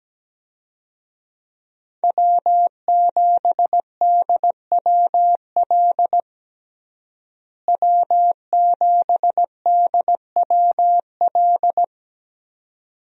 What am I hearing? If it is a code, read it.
Morse code, heard as a keyed tone, about 17 wpm, 700 Hz. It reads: W7DWL W7DWL
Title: USA Call Signs - 5 Characters - Repeated 2x - 17wpm